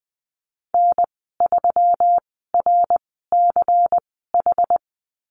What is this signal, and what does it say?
Morse code, heard as a keyed tone, about 20 wpm, spaced as for 14 wpm, 700 Hz. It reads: N3RCH
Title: 122 Lesson 20g - Number 3 - Call Signs 20wpm